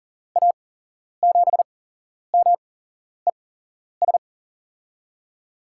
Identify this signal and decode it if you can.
Morse code, heard as a keyed tone, about 40 wpm, spaced as for 11 wpm, 700 Hz. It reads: A7MES
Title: USA Call Signs - 6 Characters - 8x Character Spacing - 40wpm